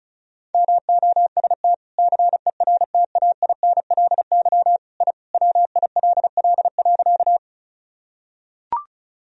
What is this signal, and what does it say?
Morse code, heard as a keyed tone, about 36 wpm, 700 Hz. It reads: MOST CERTAINLY I WILL.
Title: Sentences from Top 1000 Words - Review Part 1 - 35wpm